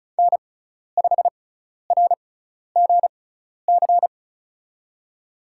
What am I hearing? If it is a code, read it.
Morse code, heard as a keyed tone, about 34 wpm, spaced as for 12 wpm, 700 Hz. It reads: N5RGC
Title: USA Call Signs - 6x Character Spacing - 35wpm